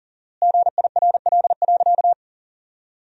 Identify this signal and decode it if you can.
Morse code, heard as a keyed tone, about 40 wpm, 700 Hz. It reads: GIRL.
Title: 4x Word-Spacing - Sentences from Top 400 Words - 40wpm